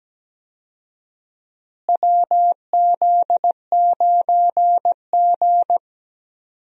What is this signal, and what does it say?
Morse code, heard as a keyed tone, about 17 wpm, 700 Hz. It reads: WZ9G
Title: USA Call Signs - 4 Characters - Repeated 3x - 17wpm